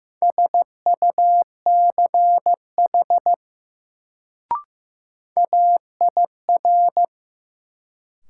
Morse code, heard as a keyed tone, about 15 wpm, 700 Hz. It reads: SUCH AIR
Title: Sentences from Top 200 Words - New Word Drill - 15wpm